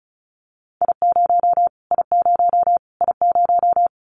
Morse code, heard as a keyed tone, about 34 wpm, 700 Hz. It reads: I0 I0 I0
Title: Callsign Prefixes for Top 25 DXCC - Repeated 3x - 35wpm